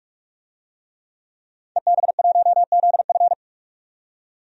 Morse code, heard as a keyed tone, about 42 wpm, 700 Hz. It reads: EB1ZF
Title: DXCC-100 Call Signs - 5 Characters - 45wpm